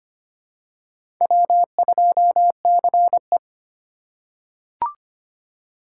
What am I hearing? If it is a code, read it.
Morse code, heard as a keyed tone, about 25 wpm, 700 Hz. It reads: W2CE
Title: USA Call Signs - 4 Characters - 3x Character Spacing - 25wpm